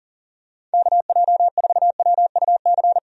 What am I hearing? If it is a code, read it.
Morse code, heard as a keyed tone, about 40 wpm, 700 Hz. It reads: KJ4WUC
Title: USA Call Signs - 6 Characters - 40wpm